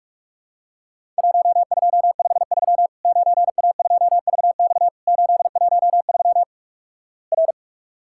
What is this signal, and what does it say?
Morse code, heard as a keyed tone, about 46 wpm, 700 Hz. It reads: 1253 9A2VX 813 R
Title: Worked All Europe DX Contest - QTC Exchange - 45wpm